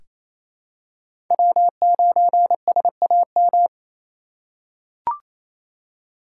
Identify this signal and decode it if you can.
Morse code, heard as a keyed tone, about 28 wpm, 700 Hz. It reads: W9SAM
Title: USA Call Signs - 5 Characters - 2x Character Spacing - 28wpm